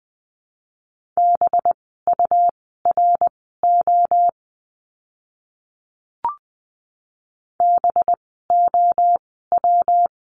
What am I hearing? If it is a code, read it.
Morse code, heard as a keyed tone, about 20 wpm, spaced as for 15 wpm, 700 Hz. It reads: BURO BOW
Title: 185 Lesson 30f - Letter b - Words 20wpm